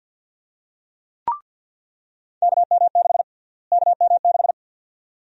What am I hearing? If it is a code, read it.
Morse code, heard as a keyed tone, about 47 wpm, 700 Hz. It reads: KM6 KM6
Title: USA Call Sign Prefixes - Repeated 2x - 50wpm